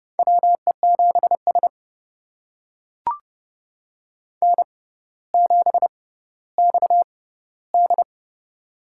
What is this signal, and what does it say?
Morse code, heard as a keyed tone, about 31 wpm, spaced as for 15 wpm, 700 Hz. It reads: WE7S N7XD
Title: USA Call Signs - 4 Characters - 6x Character Spacing - 30wpm